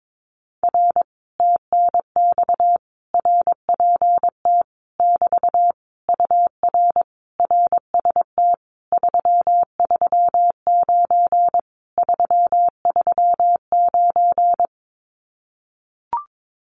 Morse code, heard as a keyed tone, about 22 wpm, 700 Hz. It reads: R TNX RPT <BT> UR RST 339 339
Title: Common QSO Phrases Encore 22wpm